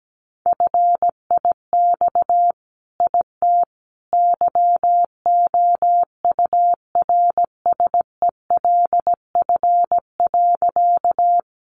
Morse code, heard as a keyed tone, about 17 wpm, 700 Hz. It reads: FIX IT YOURSELF.
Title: Sentences from Top 1400 Words 17wpm